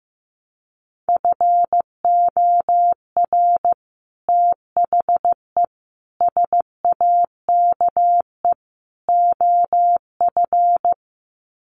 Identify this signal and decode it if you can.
Morse code, heard as a keyed tone, about 15 wpm, 700 Hz. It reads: FOR THE SAKE OF
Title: Sets of 4 Words 15wpm